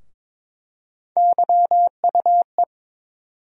Morse code, heard as a keyed tone, about 22 wpm, 700 Hz. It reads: YUE
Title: USA Call Sign Postfixes - 3 Letters 22wpm